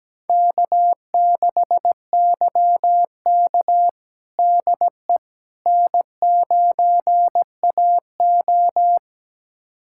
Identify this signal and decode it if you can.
Morse code, heard as a keyed tone, about 17 wpm, 700 Hz. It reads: K6YK DE N9AO
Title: USA Callsign DE Callsign - 4 Characters - 17wpm